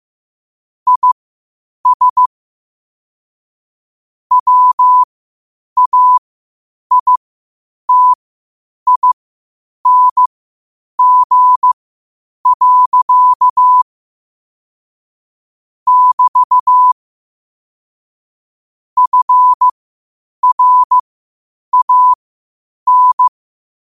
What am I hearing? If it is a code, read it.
Morse code, heard as a keyed tone, about 15 wpm, spaced as for 8 wpm, 1000 Hz. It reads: IS WAITING. <BT> FRAN
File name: qotdfiles / 8wpm